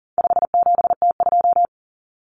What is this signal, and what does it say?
Morse code, heard as a keyed tone, about 40 wpm, 700 Hz. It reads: 57T2
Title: Common Cut Numbers - 4-Digit 40wpm